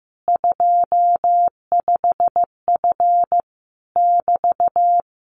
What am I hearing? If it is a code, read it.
Morse code, heard as a keyed tone, about 15 wpm, 700 Hz. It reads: 25F <BT>
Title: Common QSO Phrases Encore 15wpm